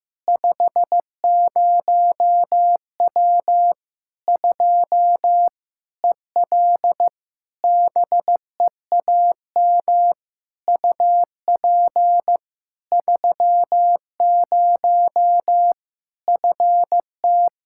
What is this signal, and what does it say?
Morse code, heard as a keyed tone, about 16 wpm, 700 Hz. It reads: 50W 2 EL BEAM UP 30 FT
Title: Practice QSO - Exchange 2 - Info - 15wpm